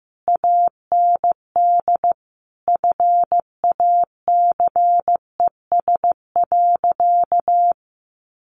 Morse code, heard as a keyed tone, about 15 wpm, 700 Hz. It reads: AND FACES.
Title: A Sound Of Thunder 15wpm